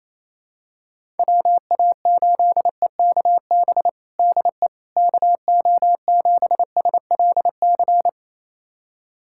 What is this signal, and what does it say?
Morse code, heard as a keyed tone, about 28 wpm, 700 Hz. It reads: WA8EKB DE KO7SLC
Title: USA Callsign DE Callsign - 6 Characters - 28wpm